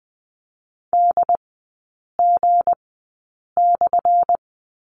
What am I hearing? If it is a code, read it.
Morse code, heard as a keyed tone, about 20 wpm, spaced as for 8 wpm, 700 Hz. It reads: DG/
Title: 178 Lesson 29e - Slash - Three Characters 20wpm